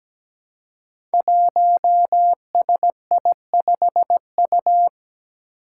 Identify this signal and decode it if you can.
Morse code, heard as a keyed tone, about 17 wpm, 700 Hz. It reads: 1SI5U
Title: Five Challenging Letter-Number Combinations 17wpm